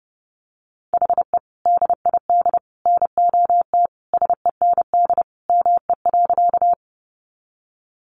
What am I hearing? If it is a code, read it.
Morse code, heard as a keyed tone, about 30 wpm, 700 Hz. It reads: HE DID NOT SEND ME.
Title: Sentences from Top 1500 Words 30wpm